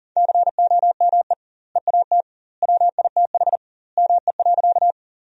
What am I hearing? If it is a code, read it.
Morse code, heard as a keyed tone, about 40 wpm, 700 Hz. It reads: COME EAT WITH ME.
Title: 2x Word-Spacing - Sentences from Top 300 Words - 40wpm